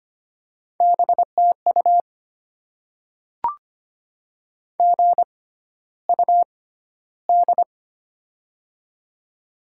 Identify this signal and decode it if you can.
Morse code, heard as a keyed tone, about 25 wpm, spaced as for 13 wpm, 700 Hz. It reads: BTU GUD
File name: Top 20 QSO Elements - 6x Character Spacing - 25wpm